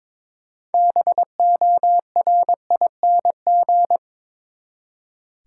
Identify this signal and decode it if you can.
Morse code, heard as a keyed tone, about 22 wpm, 700 Hz. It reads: BORING
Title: Six Letter Words - 22wpm